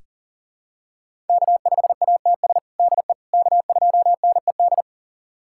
Morse code, heard as a keyed tone, about 40 wpm, 700 Hz. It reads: K5ATS DE K2NED
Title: USA Callsign DE Callsign - 5 Characters - 40wpm